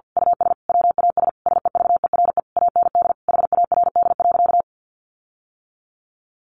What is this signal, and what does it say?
Morse code, heard as a keyed tone, about 48 wpm, 700 Hz. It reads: US WAS SEVERE AND HARD.
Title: Sentences from Top 1400 Words 50wpm